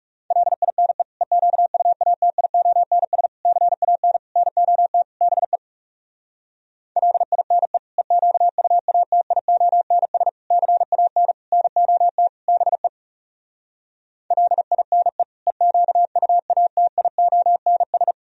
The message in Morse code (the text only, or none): LINE EQUATIONS CAN NOT BE LINE EQUATIONS CAN NOT BE LINE EQUATIONS